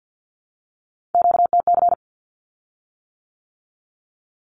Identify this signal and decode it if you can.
Morse code, heard as a keyed tone, about 50 wpm, 700 Hz. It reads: QTC
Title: Q-Codes with Spoken Meaning - 50wpm